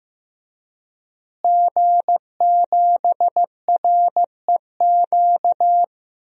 Morse code, heard as a keyed tone, about 16 wpm, 700 Hz. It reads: G7REQ
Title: DXCC-100 Call Signs - 5 Characters - Repeated 3x - 15wpm